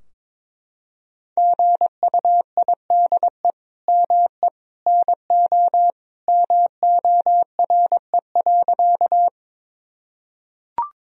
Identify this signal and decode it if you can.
Morse code, heard as a keyed tone, about 22 wpm, 700 Hz. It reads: GUIDE ME NO MORE.
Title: Sentences from Top 1000 Words - Review Part 3 - 22wpm